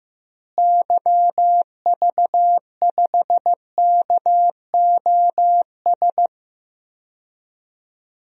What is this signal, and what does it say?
Morse code, heard as a keyed tone, about 15 wpm, 700 Hz. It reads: YV5KOS
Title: DXCC-100 Call Signs - 6 Characters - Repeated 2x - 15wpm